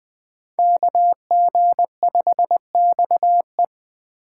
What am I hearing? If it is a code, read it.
Morse code, heard as a keyed tone, about 23 wpm, 700 Hz. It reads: KG5XE
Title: Speed Racing - USA Call Signs 20wpm